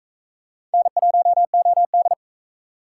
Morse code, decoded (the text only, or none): N1OD